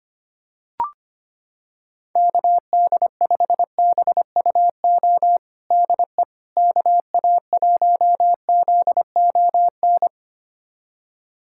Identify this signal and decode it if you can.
Morse code, heard as a keyed tone, about 25 wpm, 700 Hz. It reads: KD5BUO DE KA1ZON
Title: USA Callsign DE Callsign - 6 Characters - 25wpm